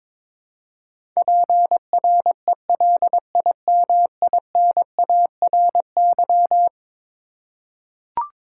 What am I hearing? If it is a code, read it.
Morse code, heard as a keyed tone, about 21 wpm, 700 Hz. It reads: PRELIMINARY
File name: practice / 22wpm / Eleven Letter Words - 22wpm